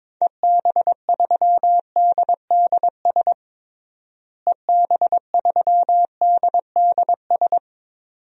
Morse code, heard as a keyed tone, about 22 wpm, 700 Hz. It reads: EB3DDS EB3DDS
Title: DXCC-100 Call Signs - 6 Characters - Repeated 2x - 22wpm